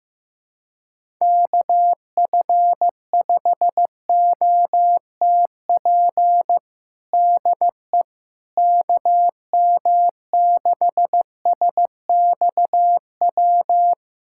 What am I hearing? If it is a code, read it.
Morse code, heard as a keyed tone, about 15 wpm, 700 Hz. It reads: KF5OTP DE KM6SXW
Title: USA Callsign DE Callsign 15wpm